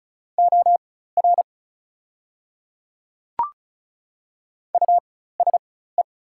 Morse code, heard as a keyed tone, about 36 wpm, spaced as for 16 wpm, 700 Hz. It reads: OR USE
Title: 4x Character Spacing - Top 200 Words - 35wpm